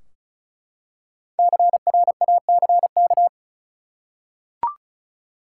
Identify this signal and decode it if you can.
Morse code, heard as a keyed tone, about 36 wpm, 700 Hz. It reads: CRACK
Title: Five Letter Words - 35wpm